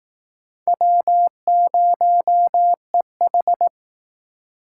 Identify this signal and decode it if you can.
Morse code, heard as a keyed tone, about 18 wpm, 700 Hz. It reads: W0EH
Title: Speed Racing - USA 1x2 Call Signs 15wpm